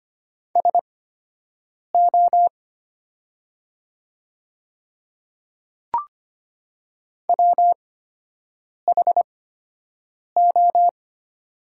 Morse code, heard as a keyed tone, about 25 wpm, spaced as for 7 wpm, 700 Hz. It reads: SO WHO
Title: 8x Character Spacing - Top 100 Words - 25wpm